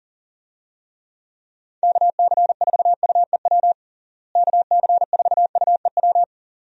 Morse code, heard as a keyed tone, about 40 wpm, 700 Hz. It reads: KC4UEW KC4UEW
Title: USA Call Signs - 6 Characters - Repeated 2x - 40wpm